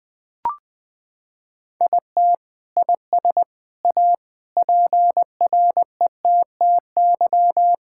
Morse code, heard as a keyed tone, about 20 wpm, 700 Hz. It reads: IT IS A PRETTY
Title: Sentences from Top 1000 Words - Review Part 6 - 20wpm